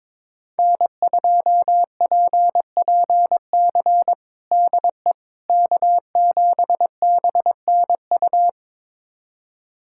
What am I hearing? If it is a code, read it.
Morse code, heard as a keyed tone, about 22 wpm, 700 Hz. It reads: N2PPC DE K7BNU
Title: USA Callsign DE Callsign - 5 Characters - 22wpm